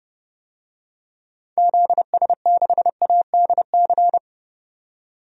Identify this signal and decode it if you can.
Morse code, heard as a keyed tone, about 31 wpm, 700 Hz. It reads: ZS6ADC
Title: DXCC-100 Call Signs - 6 Characters - 30wpm